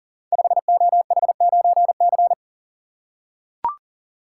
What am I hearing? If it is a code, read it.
Morse code, heard as a keyed tone, about 39 wpm, 700 Hz. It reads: 5OH9C
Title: Code Groups - 5 Letter:Numbers - 40wpm